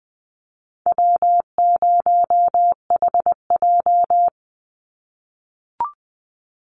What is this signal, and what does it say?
Morse code, heard as a keyed tone, about 20 wpm, 700 Hz. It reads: W0HJ
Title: USA Call Signs 20wpm